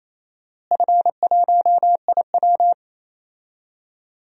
Morse code, heard as a keyed tone, about 27 wpm, 700 Hz. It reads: F1IW
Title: DXCC-100 Call Signs - 4 Characters - Repeated 3x - 28wpm